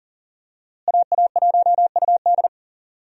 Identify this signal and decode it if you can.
Morse code, heard as a keyed tone, about 40 wpm, 700 Hz. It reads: AA1UD
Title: USA Call Signs - 5 Characters - Repeated 2x - 40wpm